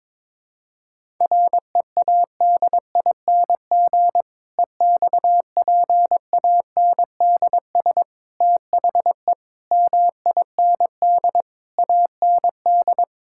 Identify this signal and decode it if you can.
Morse code, heard as a keyed tone, about 21 wpm, 700 Hz. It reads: READING EXPANDS THE MIND AND
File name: Difficult Sentences with Hints - 22wpm